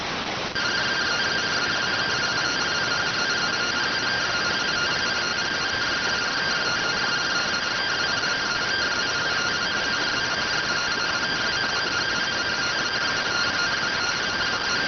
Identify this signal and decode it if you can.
Two-tone teleprinter signal, about 170 Hz shift.